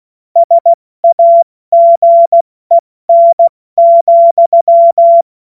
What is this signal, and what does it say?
Morse code, heard as a keyed tone, about 16 wpm, spaced as for 14 wpm, 680 Hz. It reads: SAGEN,